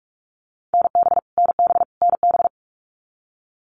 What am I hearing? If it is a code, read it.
Morse code, heard as a keyed tone, about 44 wpm, 700 Hz. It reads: NB NB NB
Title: Canadian Provinces and Territories - Repeated 3x - 45wpm